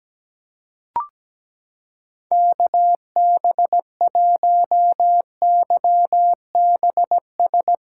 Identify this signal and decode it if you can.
Morse code, heard as a keyed tone, about 17 wpm, 700 Hz. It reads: KB1YBS
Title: USA Callsign DE Callsign - 6 Characters - 17wpm